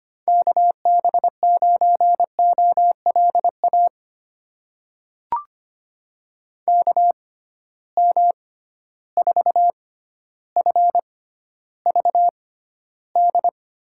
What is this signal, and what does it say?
Morse code, heard as a keyed tone, about 25 wpm, spaced as for 13 wpm, 700 Hz. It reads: KB9OLA KM4FVD
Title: USA Call Signs - 6x Character Spacing - 25wpm